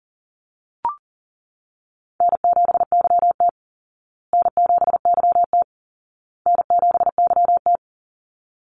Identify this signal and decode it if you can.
Morse code, heard as a keyed tone, about 37 wpm, 700 Hz. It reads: N7YT N7YT N7YT
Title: Easy Call Signs Repeated 3x - 40wpm